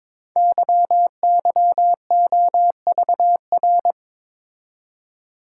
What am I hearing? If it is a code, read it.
Morse code, heard as a keyed tone, about 22 wpm, 700 Hz. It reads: YYOVR
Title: Code Groups - 5 Letters - 22wpm